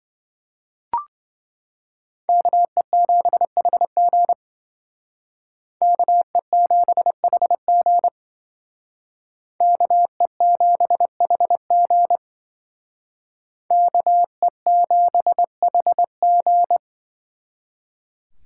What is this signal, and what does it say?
Morse code, heard as a keyed tone, about 26 wpm, 700 Hz. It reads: KE7HG KE7HG KE7HG KE7HG